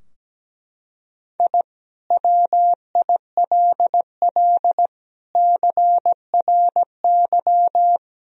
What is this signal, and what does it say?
Morse code, heard as a keyed tone, about 17 wpm, 700 Hz. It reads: I WILL CRY
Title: Sentences from Top 1000 Words - Review Part 2 - 17wpm